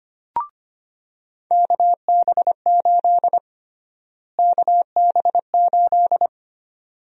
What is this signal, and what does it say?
Morse code, heard as a keyed tone, about 25 wpm, 700 Hz. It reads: KB8 KB8
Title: USA Call Sign Prefixes - Repeated 2x - 25wpm